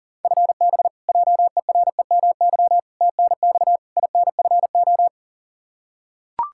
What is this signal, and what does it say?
Morse code, heard as a keyed tone, about 37 wpm, 700 Hz. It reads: FB JEREMY TNX INFO
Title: Practice QSO - Exchange 3 - More Info - 40wpm